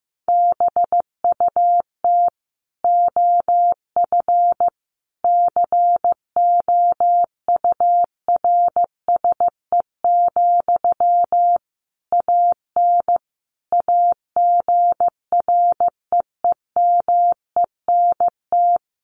Morse code, heard as a keyed tone, about 15 wpm, 700 Hz. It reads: BUT OF COURSE, AN AGREEMENT
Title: Sentences from Top 1500 Words 15wpm